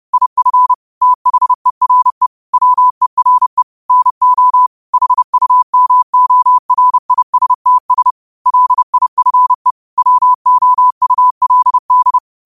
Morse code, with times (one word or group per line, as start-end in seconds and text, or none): 0.1-0.7 IF
1.0-2.3 THERE
2.5-3.6 WERE
3.9-4.7 NO
4.9-8.1 HUMORISTS
8.5-9.7 LIFE
10.0-12.2 WOULD